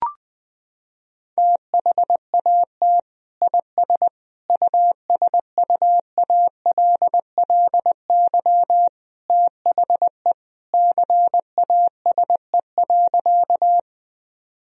Morse code, heard as a keyed tone, about 20 wpm, 700 Hz. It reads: THAT IS USUALLY THE CASE.
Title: Sentences from Top 1000 Words - Review Part 3 - 20wpm